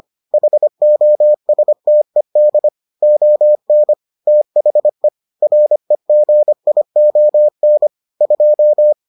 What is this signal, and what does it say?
Morse code, heard as a keyed tone, about 25 wpm, 600 Hz. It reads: HOSTED ON THE REGION 2